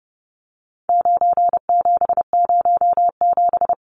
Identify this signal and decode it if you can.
Morse code, heard as a keyed tone, about 30 wpm, 700 Hz. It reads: 9707
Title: Numbers - 4-Digit 30wpm